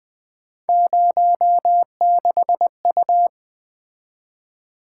Morse code, heard as a keyed tone, about 20 wpm, 700 Hz. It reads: 06U
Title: Three Letter-Number Combinations - 20wpm